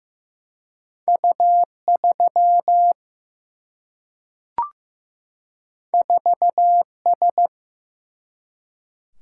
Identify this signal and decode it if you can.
Morse code, heard as a keyed tone, about 15 wpm, 700 Hz. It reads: U3 4S
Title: Single-Letter Single-Number 15wpm